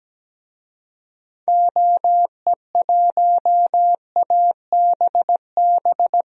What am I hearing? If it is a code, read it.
Morse code, heard as a keyed tone, about 17 wpm, 700 Hz. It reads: OE1ABB
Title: DXCC-100 Call Signs - 6 Characters - 17wpm